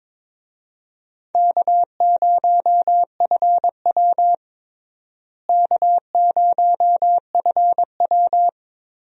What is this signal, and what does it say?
Morse code, heard as a keyed tone, about 22 wpm, 700 Hz. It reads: K0FW K0FW
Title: USA Call Signs - 4 Characters - Repeated 2x - 22wpm